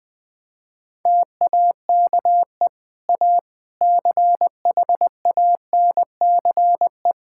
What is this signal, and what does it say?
Morse code, heard as a keyed tone, about 21 wpm, 700 Hz. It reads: TAKE A CHANCE
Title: Sentences with 3-Word Phrases - Spoken Beginning - 20wpm